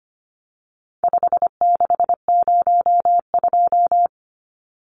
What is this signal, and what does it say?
Morse code, heard as a keyed tone, about 25 wpm, 700 Hz. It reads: 5602
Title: Numbers - 4-Digit 25wpm